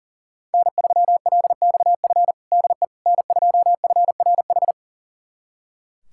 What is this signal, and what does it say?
Morse code, heard as a keyed tone, about 40 wpm, 700 Hz. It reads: N3LXF DE N2FRH
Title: USA Callsign DE Callsign - 5 Characters - 40wpm